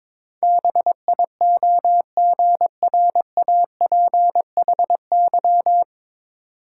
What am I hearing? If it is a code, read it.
Morse code, heard as a keyed tone, about 22 wpm, 700 Hz. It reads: BIOGRAPHY
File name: 9 Letter Words - First 3 Letters Given - 22wpm